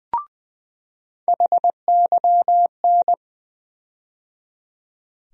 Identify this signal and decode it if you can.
Morse code, heard as a keyed tone, about 20 wpm, 700 Hz. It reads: HYN
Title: USA Call Sign Postfixes - 3 Letters 20wpm